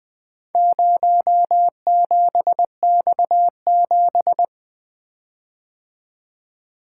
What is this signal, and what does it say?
Morse code, heard as a keyed tone, about 24 wpm, 700 Hz. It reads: 07X7
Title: Speed-Racing - Four Letter-Number Combinations - 20wpm